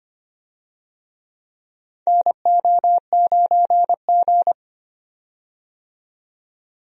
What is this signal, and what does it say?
Morse code, heard as a keyed tone, about 25 wpm, 700 Hz. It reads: NO9G
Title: USA Call Signs - 4 Characters - 25wpm